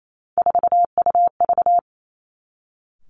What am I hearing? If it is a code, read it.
Morse code, heard as a keyed tone, about 28 wpm, 700 Hz. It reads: 4UV
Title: Three Challenging Letter-Number Combinations 28wpm